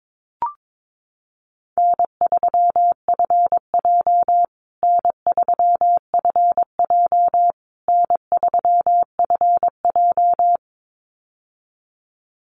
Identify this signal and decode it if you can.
Morse code, heard as a keyed tone, about 22 wpm, 700 Hz. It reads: N3FJ N3FJ N3FJ
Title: USA Call Signs Repeated 3x 22wpm